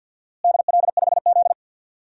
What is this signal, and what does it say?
Morse code, heard as a keyed tone, about 47 wpm, 700 Hz. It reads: DL5Z
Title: DXCC-100 Call Signs - 4 Characters - Repeated 2x - 50wpm